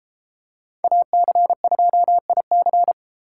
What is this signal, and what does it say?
Morse code, heard as a keyed tone, about 33 wpm, 700 Hz. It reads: AC2IC